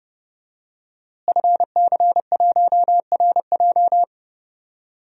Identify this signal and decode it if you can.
Morse code, heard as a keyed tone, about 30 wpm, 700 Hz. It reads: FC1RJ
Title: DXCC-100 Call Signs - 5 Characters - Repeated 2x - 30wpm